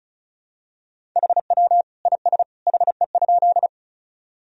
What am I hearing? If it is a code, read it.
Morse code, heard as a keyed tone, about 34 wpm, 700 Hz. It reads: HW IS HE?
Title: Sentences from Top 100 Words and Abbreviations 35wpm